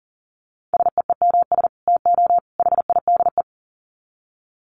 Morse code, heard as a keyed tone, about 40 wpm, 700 Hz. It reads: SEEMS TO HIDE
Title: Dit Challenge 40wpm